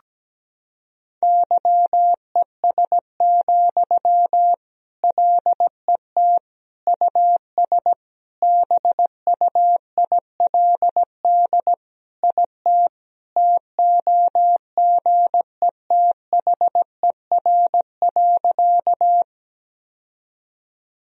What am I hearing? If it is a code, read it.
Morse code, heard as a keyed tone, about 17 wpm, 700 Hz. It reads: YES, LET US BUILD IT TOGETHER.
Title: Sentences from Top 1000 Words - Review Part 1 - 17wpm